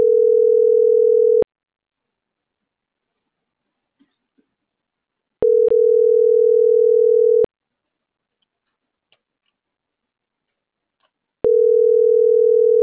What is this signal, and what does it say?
Ringback tone (North America), 440 + 480 Hz.